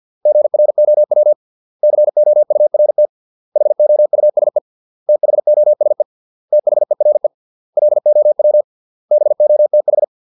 Morse code, heard as a keyed tone, about 50 wpm, 600 Hz. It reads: GROW COURT HOUSE THOSE THERE LOW BOTH